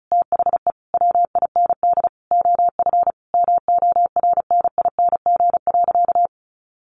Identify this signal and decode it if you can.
Morse code, heard as a keyed tone, about 34 wpm, 700 Hz. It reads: THE WIND OF MORNING.
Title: A Sound Of Thunder 35wpm